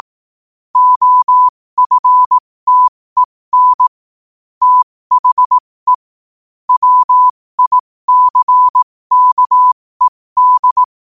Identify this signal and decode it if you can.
Morse code, heard as a keyed tone, about 18 wpm, spaced as for 15 wpm, 1000 Hz. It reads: OFTEN THE WICKED
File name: qotdfiles / 15wpm